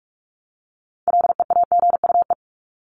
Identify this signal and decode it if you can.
Morse code, heard as a keyed tone, about 44 wpm, 700 Hz. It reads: LEAGUE
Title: Top 1751-2000 Words 45wpm